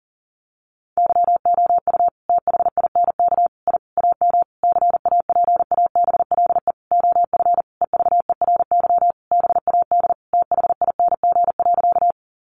Morse code, heard as a keyed tone, about 40 wpm, 700 Hz. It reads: YOU THINK I AM CAPABLE OF EVERY BAD THING.
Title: Sentences from Top 1200 Words 40wpm